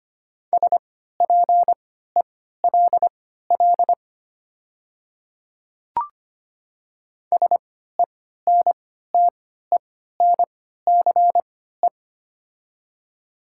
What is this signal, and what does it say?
Morse code, heard as a keyed tone, about 25 wpm, spaced as for 14 wpm, 700 Hz. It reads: SPELL SENTENCE
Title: 3x Character Spacing - Top 200 Words - 25wpm